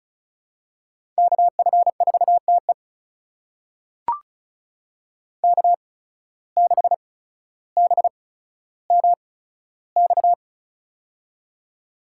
Morse code, heard as a keyed tone, about 36 wpm, spaced as for 15 wpm, 700 Hz. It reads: KF4TE K6BMX
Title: USA Call Signs - 5 Characters - 8x Character Spacing - 35wpm